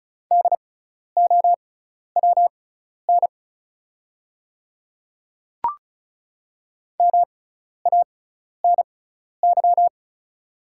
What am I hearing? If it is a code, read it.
Morse code, heard as a keyed tone, about 36 wpm, spaced as for 12 wpm, 700 Hz. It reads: DOWN MANY
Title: 6x Character Spacing - Top 100 Words - 35wpm